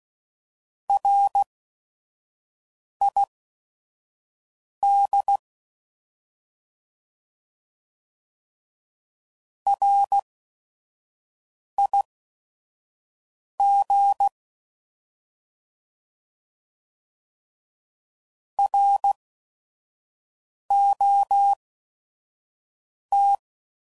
Morse code, heard as a keyed tone, about 16 wpm, spaced as for 5 wpm, 800 Hz. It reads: RID RIG ROT
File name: Prefixes / By First Letter / Rs